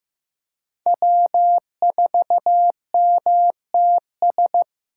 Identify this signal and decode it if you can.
Morse code, heard as a keyed tone, about 18 wpm, 700 Hz. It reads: W4MTS
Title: Speed Racing - USA 1x3 Call Signs 15wpm